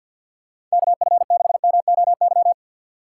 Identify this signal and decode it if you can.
Morse code, heard as a keyed tone, about 49 wpm, 700 Hz. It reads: K F 6 M O Y
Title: USA Call Signs - 6 Characters - 6x Character Spacing - 50wpm